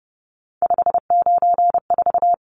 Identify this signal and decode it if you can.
Morse code, heard as a keyed tone, about 30 wpm, 700 Hz. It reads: 594
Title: Numbers - 3-Digit 30wpm